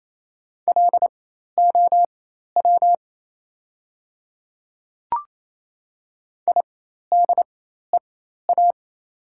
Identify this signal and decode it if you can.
Morse code, heard as a keyed tone, about 28 wpm, spaced as for 13 wpm, 700 Hz. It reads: LOW IDEA
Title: 4x Character Spacing - Top 300 Words - 28wpm